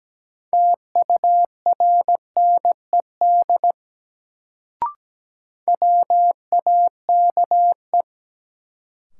Morse code, heard as a keyed tone, about 17 wpm, 700 Hz. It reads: TURNED WAKE
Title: Sets of 4 Words - New Word Drill - 17wpm